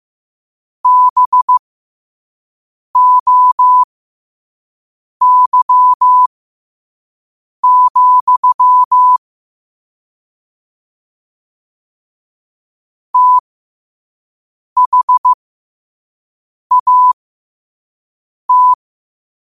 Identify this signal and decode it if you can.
Morse code, heard as a keyed tone, about 15 wpm, spaced as for 5 wpm, 1000 Hz. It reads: BOY, THAT
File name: qotdfiles / 5wpm